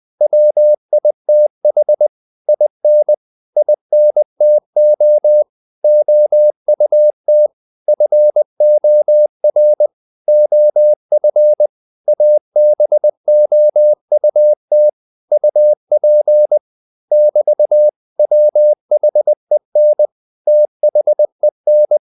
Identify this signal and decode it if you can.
Morse code, heard as a keyed tone, about 20 wpm, 600 Hz. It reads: WITH IN INTO OUT FOR OF ABOUT UP <BT>WHEN THEN
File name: CW / Practice / 500MostCommonEnglishWords20